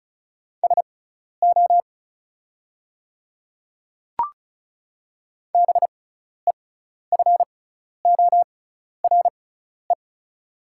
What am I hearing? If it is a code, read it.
Morse code, heard as a keyed tone, about 36 wpm, spaced as for 12 wpm, 690 Hz. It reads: SO BEFORE
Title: 6x Character Spacing - Top 300 Words - 35wpm